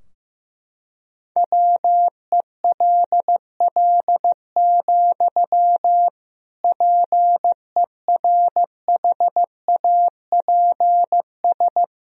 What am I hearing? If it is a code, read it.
Morse code, heard as a keyed tone, about 15 wpm, 700 Hz. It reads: WELL, PERHAPS
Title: Sentences from Top 1000 Words - Review Part 2 - 15wpm